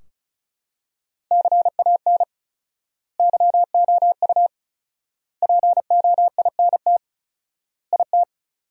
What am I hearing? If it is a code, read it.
Morse code, heard as a keyed tone, about 34 wpm, 700 Hz. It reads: CAN YOU POINT IT
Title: 4x Word-Spacing - Sentences from Top 200 Words - 35wpm